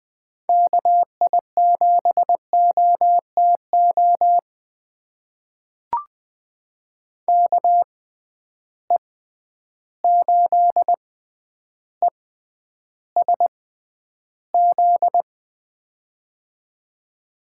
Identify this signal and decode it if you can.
Morse code, heard as a keyed tone, about 20 wpm, spaced as for 10 wpm, 700 Hz. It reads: KI7OTO KE8ESZ
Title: USA Call Signs - 6x Character Spacing - 20wpm